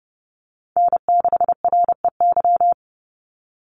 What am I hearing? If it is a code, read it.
Morse code, heard as a keyed tone, about 30 wpm, 700 Hz. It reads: N6REY
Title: USA Call Signs 30wpm